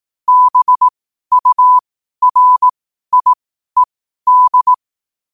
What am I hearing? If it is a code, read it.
Morse code, heard as a keyed tone, about 18 wpm, spaced as for 12 wpm, 1000 Hz. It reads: BURIED
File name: qotdfiles / 12wpm